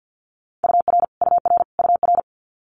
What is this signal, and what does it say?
Morse code, heard as a keyed tone, about 50 wpm, 700 Hz. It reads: UR UR UR
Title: Multi-letter Phonemes Repeated 3x 50wpm